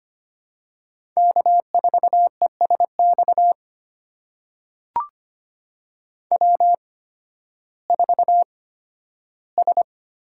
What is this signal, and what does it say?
Morse code, heard as a keyed tone, about 25 wpm, spaced as for 13 wpm, 700 Hz. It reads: K4ESX W4S
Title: USA Call Signs - 5 Characters - 8x Character Spacing - 25wpm